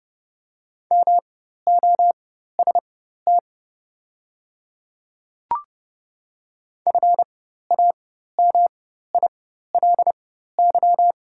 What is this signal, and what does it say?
Morse code, heard as a keyed tone, about 30 wpm, spaced as for 14 wpm, 700 Hz. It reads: MOST FAMILY
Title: 4x Character Spacing - Top 500 Words - 30wpm